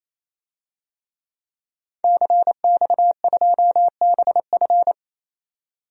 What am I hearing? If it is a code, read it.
Morse code, heard as a keyed tone, about 27 wpm, 700 Hz. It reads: CX2BF
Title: DXCC-100 Call Signs - 5 Characters - 28wpm